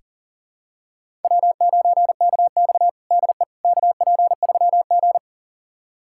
Morse code, heard as a keyed tone, about 40 wpm, 700 Hz. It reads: W9KX DE KP3G
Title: USA Callsign DE Callsign - 4 Characters - 40wpm